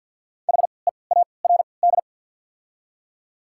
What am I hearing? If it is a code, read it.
Morse code, heard as a keyed tone, about 54 wpm, spaced as for 29 wpm, 700 Hz. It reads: HEARD
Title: 3x Character Spacing - Top 500 Words - 50wpm